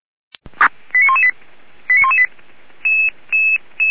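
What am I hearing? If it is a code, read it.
Five-tone selective call (ZVEI), heard as quick runs of five tones, tones of about 70 ms. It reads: 89118 89118
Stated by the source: Signal received via SDR, ZVEI five-tone.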